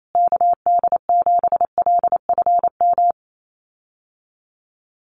Morse code, heard as a keyed tone, about 28 wpm, 700 Hz. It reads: KD7LFM
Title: USA Call Signs Repeated 2x 28wpm